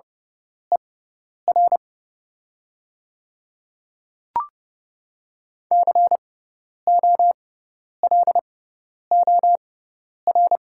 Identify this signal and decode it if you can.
Morse code, heard as a keyed tone, about 30 wpm, spaced as for 10 wpm, 700 Hz. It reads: ER COLOR
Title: 6x Character Spacing - Top 300 Words - 30wpm